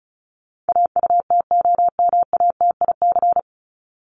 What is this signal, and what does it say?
Morse code, heard as a keyed tone, about 34 wpm, 700 Hz. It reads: AUTOMATIC
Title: Top 1751-2000 Words 35wpm